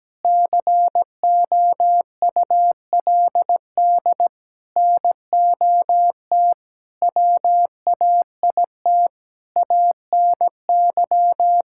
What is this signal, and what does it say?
Morse code, heard as a keyed tone, about 17 wpm, 700 Hz. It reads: COULD NOT WAIT ANY
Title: Sentences from Top 1000 Words - Review Part 5 - 17wpm